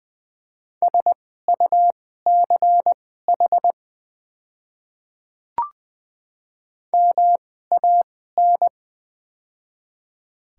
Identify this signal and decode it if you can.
Morse code, heard as a keyed tone, about 20 wpm, spaced as for 14 wpm, 700 Hz. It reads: SUCH MAN
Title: 2x Character Spacing - Top 300 Words - 20wpm